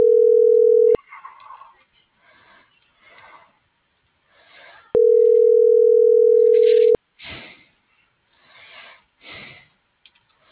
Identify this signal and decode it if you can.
Ringback tone (North America), 440 + 480 Hz.